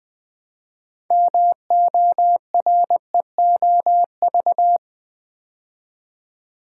Morse code, heard as a keyed tone, about 20 wpm, 700 Hz. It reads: MOREOV
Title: Build A Word - 3 Letters At A Time - 8-Letter Words - 20wpm